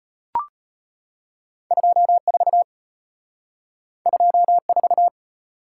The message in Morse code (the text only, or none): 24 24